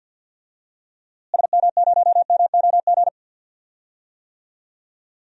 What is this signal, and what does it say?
Morse code, heard as a keyed tone, about 47 wpm, 700 Hz. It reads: SM0MOG
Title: DXCC-100 Call Signs - 6 Characters - Repeated 2x - 50wpm